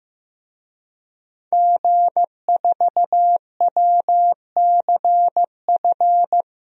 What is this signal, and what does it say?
Morse code, heard as a keyed tone, about 16 wpm, 700 Hz. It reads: G4WCF
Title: DXCC-100 Call Signs - 5 Characters - Repeated 3x - 15wpm